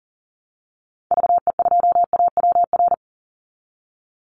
Morse code, heard as a keyed tone, about 40 wpm, 700 Hz. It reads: VE2AWR
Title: Call Signs Repeated 3x - 40wpm